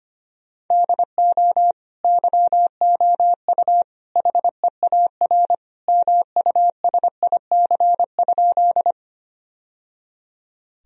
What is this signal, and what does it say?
Morse code, heard as a keyed tone, about 25 wpm, 700 Hz. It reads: DO YOU HEAR MUSIC?
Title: Sentences from Top 1000 Words - Review Part 1 - 25wpm